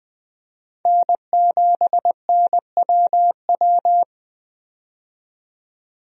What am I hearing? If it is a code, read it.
Morse code, heard as a keyed tone, about 20 wpm, 700 Hz. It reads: N7NWW
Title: USA Call Signs - 5 Characters - 20wpm